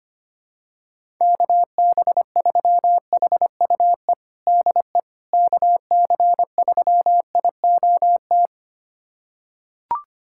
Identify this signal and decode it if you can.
Morse code, heard as a keyed tone, about 25 wpm, 700 Hz. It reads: KB3HUE DE KC3IOT
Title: USA Callsign DE Callsign - 6 Characters - 25wpm